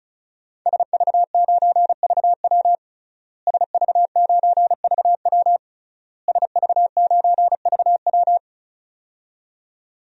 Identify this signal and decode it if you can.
Morse code, heard as a keyed tone, about 36 wpm, 700 Hz. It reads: SV9VW SV9VW SV9VW
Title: DXCC-100 Call Signs - 5 Characters - Repeated 3x - 35wpm